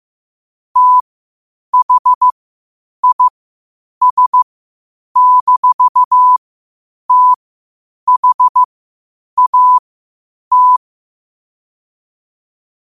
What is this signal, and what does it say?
Morse code, heard as a keyed tone, about 15 wpm, spaced as for 8 wpm, 1000 Hz. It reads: THIS-THAT
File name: qotdfiles / 8wpm